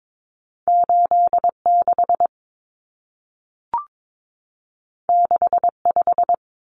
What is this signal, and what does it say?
Morse code, heard as a keyed tone, about 22 wpm, 700 Hz. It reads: 86 65
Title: Numbers - 2-Digit 22wpm